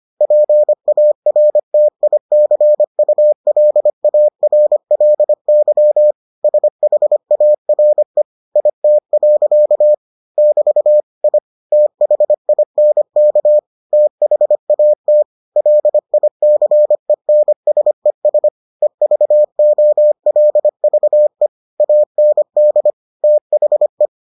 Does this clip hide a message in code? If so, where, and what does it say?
Yes: Morse code from 0.2 to 24.1 s, heard as a keyed tone, about 25 wpm, 600 Hz. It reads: PARTICULARLY SHARE IT. <BT> I THINK THAT LICENSES EVOLVE AND THE